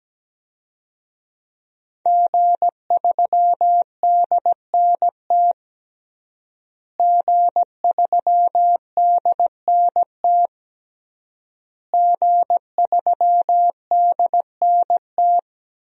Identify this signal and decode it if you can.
Morse code, heard as a keyed tone, about 17 wpm, 700 Hz. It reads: G3DNT G3DNT G3DNT
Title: DXCC-100 Call Signs - 5 Characters - Repeated 3x - 17wpm